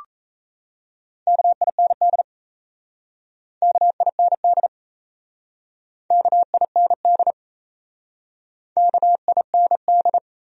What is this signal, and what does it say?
Morse code, heard as a keyed tone, about 37 wpm, 700 Hz. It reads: KIND KIND KIND KIND